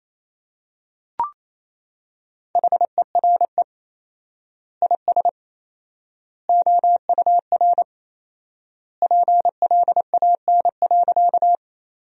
Morse code, heard as a keyed tone, about 28 wpm, 700 Hz. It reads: HERE IS OUR PLAN.
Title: 4x Word-Spacing - Sentences from Top 500 Words - 28wpm